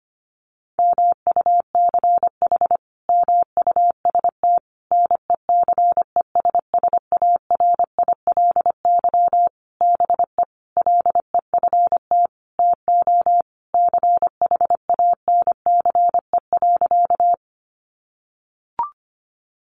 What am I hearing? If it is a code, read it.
Morse code, heard as a keyed tone, about 25 wpm, 700 Hz. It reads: MUCH MUST NECESSARILY BE LEFT TO CHANCE.
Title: Sentences from Top 1400 Words 25wpm